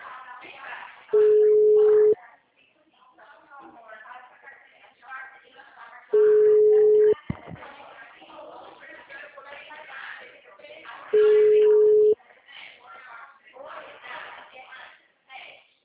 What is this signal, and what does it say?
Ringback tone (Europe, 425 Hz), with speech.